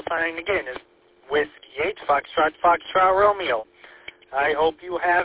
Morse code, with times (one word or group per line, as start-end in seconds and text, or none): none